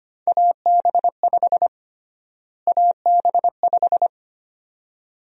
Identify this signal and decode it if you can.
Morse code, heard as a keyed tone, about 25 wpm, 700 Hz. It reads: AB5 AB5
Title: USA Call Sign Prefixes - Repeated 2x - 25wpm